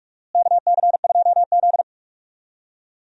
Morse code, heard as a keyed tone, about 46 wpm, 700 Hz. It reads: KC2Z
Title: USA Call Signs - 4 Characters - Repeated 2x - 45wpm